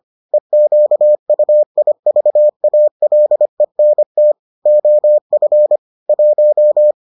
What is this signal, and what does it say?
Morse code, heard as a keyed tone, about 25 wpm, 600 Hz. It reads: EQUIVALENT OF 1